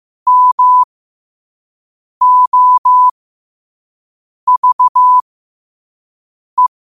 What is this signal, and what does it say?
Morse code, heard as a keyed tone, about 15 wpm, spaced as for 5 wpm, 1000 Hz. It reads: MOVE
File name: qotdfiles / 5wpm